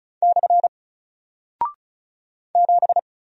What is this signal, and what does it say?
Morse code, heard as a keyed tone, about 36 wpm, 700 Hz. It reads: / 7
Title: Single Character (Letter:Number:Punctuation:Prosign) 35wpm